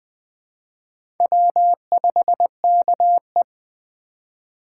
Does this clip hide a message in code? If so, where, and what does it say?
Yes: Morse code from 1.2 to 3.4 s, heard as a keyed tone, about 21 wpm, 700 Hz. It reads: W5KE